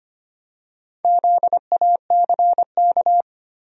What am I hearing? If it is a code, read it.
Morse code, heard as a keyed tone, about 25 wpm, 700 Hz. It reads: ZACK
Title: USA Common Names - Max 4 Letters - 25wpm